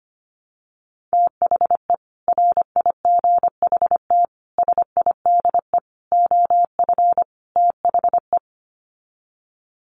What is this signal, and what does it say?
Morse code, heard as a keyed tone, about 25 wpm, 700 Hz. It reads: THE RIGHT SIDE OF THE
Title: Sets of 5 Words Encore 25wpm